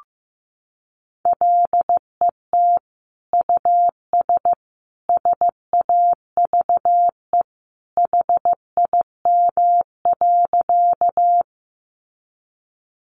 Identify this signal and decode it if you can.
Morse code, heard as a keyed tone, about 15 wpm, 700 Hz. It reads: LET US SAVE HIM.
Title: Sentences from Top 1100 Words 15wpm